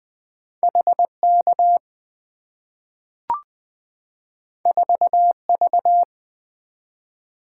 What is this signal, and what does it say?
Morse code, heard as a keyed tone, about 20 wpm, 690 Hz. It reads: HK 4V
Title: Two Letter-Number Combinations - 20wpm